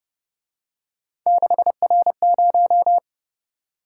Morse code, heard as a keyed tone, about 31 wpm, 700 Hz. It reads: 6R0